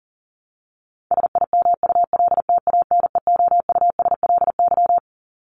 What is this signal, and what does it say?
Morse code, heard as a keyed tone, about 40 wpm, 700 Hz. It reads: SIMULTANEOUSLY
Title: LY Suffix - Top 100 Words with Base Words 40wpm